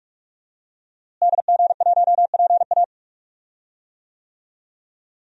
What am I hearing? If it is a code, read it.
Morse code, heard as a keyed tone, about 46 wpm, 700 Hz. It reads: DG1PA
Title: DXCC-100 Call Signs - 5 Characters - Repeated 2x - 45wpm